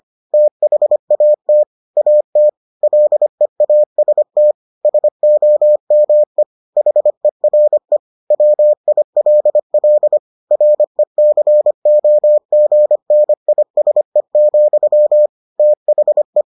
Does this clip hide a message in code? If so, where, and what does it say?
Yes: Morse code from 0.3 to 16.4 s, heard as a keyed tone, about 25 wpm, 600 Hz. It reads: THAT AT LEAST SOME HERE WILL RECOGNISE, THE